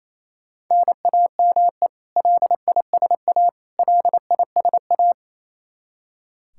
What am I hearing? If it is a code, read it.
Morse code, heard as a keyed tone, about 28 wpm, 700 Hz. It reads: NAME LISA LISA
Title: Practice QSO - Exchange 1 - Beginning - 28wpm